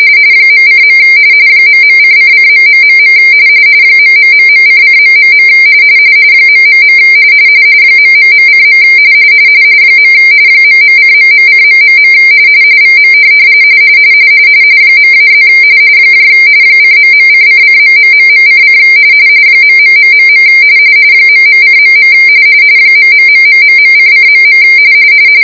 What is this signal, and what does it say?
RTTY, about 45.45 baud. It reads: .:. ..:::.... ....::... ... ...:. ..:..:... VM .-+ :..:::....:: ..... ..:.. ......::..